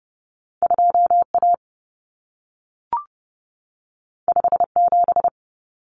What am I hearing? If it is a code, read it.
Morse code, heard as a keyed tone, about 30 wpm, 700 Hz. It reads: 2A 57
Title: Common Cut Numbers - 2-Digit 30wpm